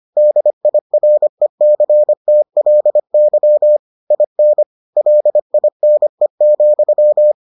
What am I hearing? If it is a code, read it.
Morse code, heard as a keyed tone, about 25 wpm, 600 Hz. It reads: DIRECTLY IN LINE,